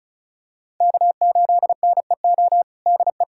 Morse code, heard as a keyed tone, about 36 wpm, 700 Hz. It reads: K8NEO DE
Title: USA Callsign DE Callsign 35wpm